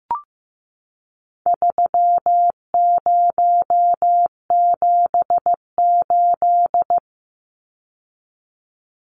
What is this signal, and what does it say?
Morse code, heard as a keyed tone, about 15 wpm, 700 Hz. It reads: 3078
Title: Numbers - 4-Digit 15wpm